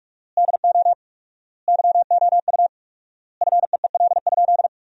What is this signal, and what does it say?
Morse code, heard as a keyed tone, about 46 wpm, 700 Hz. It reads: DO YOU FEEL?
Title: 4x Word-Spacing - Sentences from Top 400 Words - 45wpm